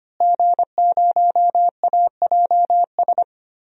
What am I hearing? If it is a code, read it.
Morse code, heard as a keyed tone, about 25 wpm, 700 Hz. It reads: G0AJS
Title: DXCC-100 Call Signs - 5 Characters - Repeated 3x - 25wpm